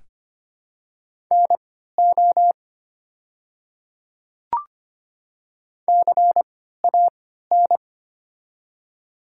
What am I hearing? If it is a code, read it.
Morse code, heard as a keyed tone, about 25 wpm, spaced as for 14 wpm, 700 Hz. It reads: NO CAN
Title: 3x Character Spacing - Top 100 Words - 25wpm